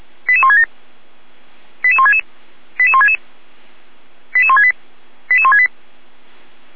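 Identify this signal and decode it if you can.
Five-tone selective call (ZVEI), heard as quick runs of five tones, tones of about 80 ms. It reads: 88167 88166 88166 88168 88168
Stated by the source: ZVEI five-tone selective call received via SDR.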